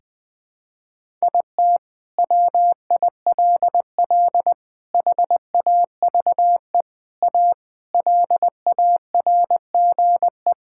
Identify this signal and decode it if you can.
Morse code, heard as a keyed tone, about 20 wpm, 700 Hz. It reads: IT WILL HAVE A LARGE
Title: Sentences from Top 1000 Words - Review Part 6 - 20wpm